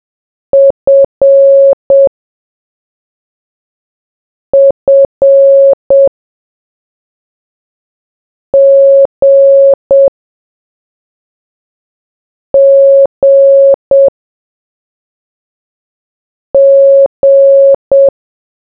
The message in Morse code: F F G G G